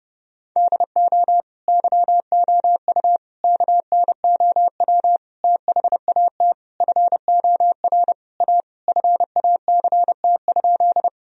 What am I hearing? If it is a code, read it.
Morse code, heard as a keyed tone, about 30 wpm, 700 Hz. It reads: DO YOU KNOW THAT FOR A FACT?
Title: Sentences from Top 1000 Words - Review Part 3 - 30wpm